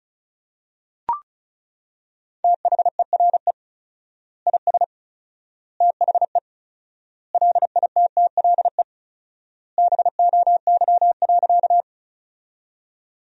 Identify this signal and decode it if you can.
Morse code, heard as a keyed tone, about 34 wpm, 700 Hz. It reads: THERE IS THE LITTLE BOY.
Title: 4x Word-Spacing - Sentences from Top 200 Words - 35wpm